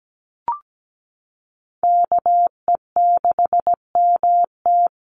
Morse code, heard as a keyed tone, about 17 wpm, 700 Hz. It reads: KE6MT
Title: NA SOTA Chaser Callsigns 17wpm